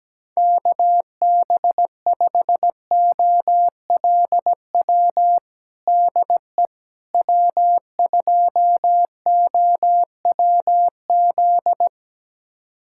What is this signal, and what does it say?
Morse code, heard as a keyed tone, about 17 wpm, 700 Hz. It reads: KB5OLW DE W2OWZ
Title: USA Callsign DE Callsign 17wpm